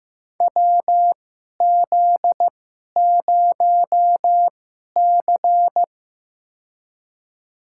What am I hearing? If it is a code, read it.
Morse code, heard as a keyed tone, about 15 wpm, spaced as for 11 wpm, 700 Hz. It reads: WZ0C
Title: USA Call Signs - 4 Characters - 2x Character Spacing - 15wpm